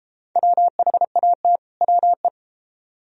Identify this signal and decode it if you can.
Morse code, heard as a keyed tone, about 33 wpm, 700 Hz. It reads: WHAT WE